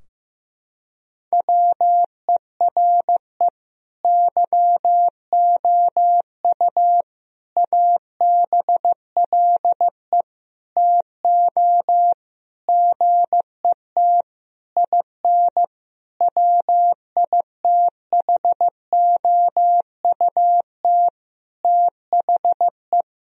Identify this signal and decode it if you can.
Morse code, heard as a keyed tone, about 15 wpm, 700 Hz. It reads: WERE YOU ABLE TO GET IN WITHOUT THE
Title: Sentences from Top 1000 Words - Review Part 4 - 15wpm